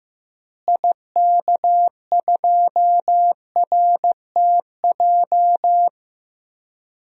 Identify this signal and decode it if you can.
Morse code, heard as a keyed tone, about 15 wpm, 700 Hz. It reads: IK2RTJ
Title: DXCC-100 Call Signs - 6 Characters - Repeated 2x - 15wpm